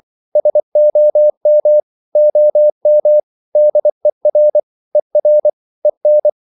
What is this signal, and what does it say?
Morse code, heard as a keyed tone, about 24 wpm, 600 Hz. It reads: SOM OM DER ER EN
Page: Ferie 24 wpm